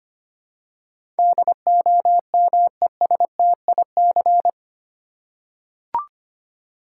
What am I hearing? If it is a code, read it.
Morse code, heard as a keyed tone, about 25 wpm, 700 Hz. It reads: DOMESTIC
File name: 8 Letter Words - First 3 Letters Given - 25wpm